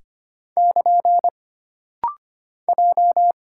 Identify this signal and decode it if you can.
Morse code, heard as a keyed tone, about 25 wpm, 700 Hz. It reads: <KN> J
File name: Single Character (Letter:Number:Punctuation:Prosign) - Rapid-Fire - 25wpm